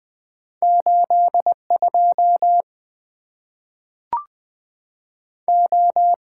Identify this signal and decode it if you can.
Morse code, heard as a keyed tone, about 20 wpm, spaced as for 3 wpm, 700 Hz. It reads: 8 2O
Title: Two Letter-Number Combinations - 20wpm